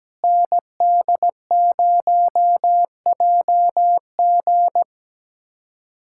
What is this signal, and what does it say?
Morse code, heard as a keyed tone, about 17 wpm, 700 Hz. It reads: ND0JG
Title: USA Callsign DE Callsign - 5 Characters - 17wpm